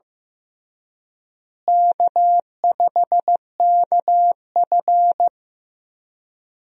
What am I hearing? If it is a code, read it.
Morse code, heard as a keyed tone, about 15 wpm, 700 Hz. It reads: K5KF
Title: USA Call Signs - 4 Characters - Repeated 3x - 15wpm